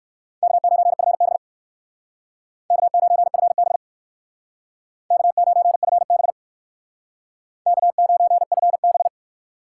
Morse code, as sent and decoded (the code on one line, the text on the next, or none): -.- ----. ..-. -... / -.- ----. ..-. -... / -.- ----. ..-. -... / -.- ----. ..-. -...
K9FB K9FB K9FB K9FB